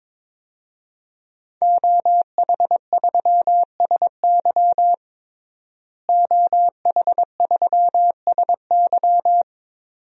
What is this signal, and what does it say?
Morse code, heard as a keyed tone, about 22 wpm, 700 Hz. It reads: OH3SY OH3SY
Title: DXCC-100 Call Signs - 5 Characters - Repeated 2x - 22wpm